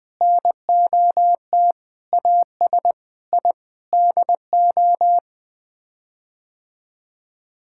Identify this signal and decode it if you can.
Morse code, heard as a keyed tone, about 21 wpm, 700 Hz. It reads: NOT AS I DO
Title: Sets of 8 Words - 20wpm